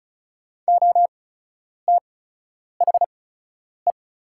Morse code, heard as a keyed tone, about 36 wpm, spaced as for 10 wpm, 700 Hz. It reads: OTHE
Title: 8x Character Spacing - Top 300 Words - 35wpm